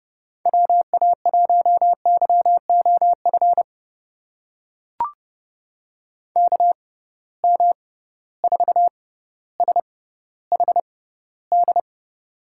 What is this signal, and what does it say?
Morse code, heard as a keyed tone, about 30 wpm, spaced as for 15 wpm, 700 Hz. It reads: WA1YOF KM4SHD
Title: USA Call Signs - 6 Characters - 6x Character Spacing - 30wpm